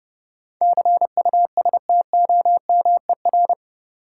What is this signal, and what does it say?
Morse code, heard as a keyed tone, about 30 wpm, 700 Hz. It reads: CUSTOMER
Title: Sentences from Top 1100 Words - New Word Drill - 30wpm